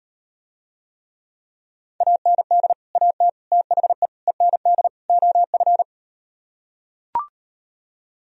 Morse code, heard as a keyed tone, about 39 wpm, 700 Hz. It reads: AND AT THE END OF